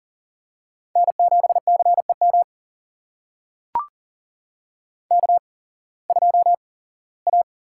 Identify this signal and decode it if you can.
Morse code, heard as a keyed tone, about 40 wpm, spaced as for 21 wpm, 700 Hz. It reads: N7CEM K2A
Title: USA Call Signs - 5 Characters - 8x Character Spacing - 40wpm